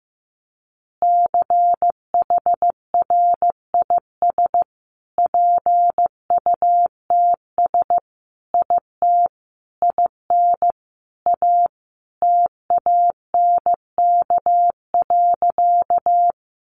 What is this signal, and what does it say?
Morse code, heard as a keyed tone, about 15 wpm, 700 Hz. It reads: CHRIS PUTS IT IN A TANK.
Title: Easy Phrases 15wpm